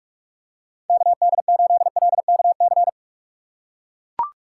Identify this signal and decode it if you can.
Morse code, heard as a keyed tone, about 41 wpm, 700 Hz. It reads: KD8LKC
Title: USA Call Signs - 6 Characters - 4x Character Spacing - 45wpm